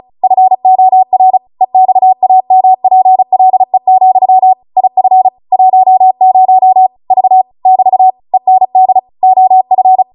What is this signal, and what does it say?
Morse code, heard as a keyed tone, about 36 wpm, 750 Hz. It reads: FOR EXAMPLE, IF 10 V <BT> END OF